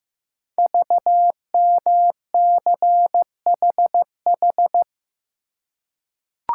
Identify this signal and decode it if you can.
Morse code, heard as a keyed tone, about 15 wpm, 700 Hz. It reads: VMCHH
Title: Code Groups - 5 Letter:Numbers - 15wpm